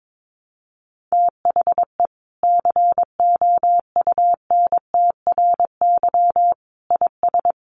Morse code, heard as a keyed tone, about 22 wpm, 700 Hz. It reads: THE COUNTRY IS
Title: Sentences from Top 1100 Words 22wpm